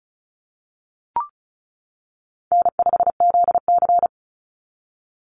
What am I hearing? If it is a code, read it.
Morse code, heard as a keyed tone, about 34 wpm, 700 Hz. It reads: N5ZC
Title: NA SOTA Chaser Callsigns 35wpm